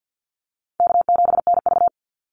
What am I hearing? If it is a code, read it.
Morse code, heard as a keyed tone, about 50 wpm, 700 Hz. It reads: K7NV
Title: Easy Call Signs - 50wpm